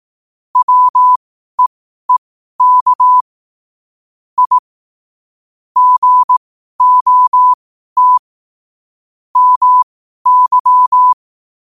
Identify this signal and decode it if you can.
Morse code, heard as a keyed tone, about 18 wpm, spaced as for 12 wpm, 1000 Hz. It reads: WEEK I GOT MY
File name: qotdfiles / 12wpm